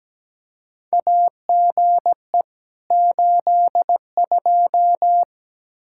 Morse code, heard as a keyed tone, about 17 wpm, 700 Hz. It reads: AGE 82
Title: K1USN SST QSO Practice 17wpm